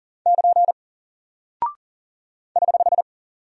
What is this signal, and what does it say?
Morse code, heard as a keyed tone, about 40 wpm, 700 Hz. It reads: <KN> <HH>
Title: Single Character (Letter:Number:Punctuation:Prosign) Alt 40wpm